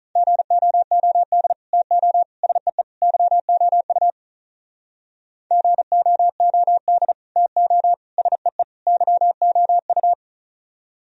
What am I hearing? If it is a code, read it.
Morse code, heard as a keyed tone, about 44 wpm, 700 Hz. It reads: GOOD TO SEE YOU GOOD TO SEE YOU